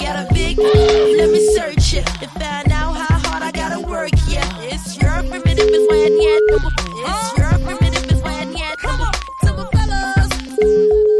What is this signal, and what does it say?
Ringback tone (Europe, 425 Hz), with speech.